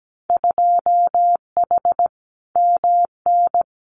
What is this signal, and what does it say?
Morse code, heard as a keyed tone, about 17 wpm, 700 Hz. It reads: 2H MN
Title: Winter Field Day Exchanges 17wpm